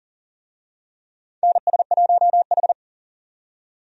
Morse code, heard as a keyed tone, about 40 wpm, 700 Hz. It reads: NS1H
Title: USA Call Signs - 4 Characters - 40wpm